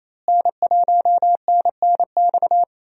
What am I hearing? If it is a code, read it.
Morse code, heard as a keyed tone, about 28 wpm, 700 Hz. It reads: N1NNX
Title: USA Call Signs - 5 Characters - Repeated 3x - 28wpm